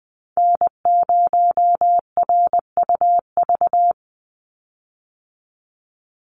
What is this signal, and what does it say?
Morse code, heard as a keyed tone, about 20 wpm, 700 Hz. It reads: N0RUV
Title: USA Call Signs 20wpm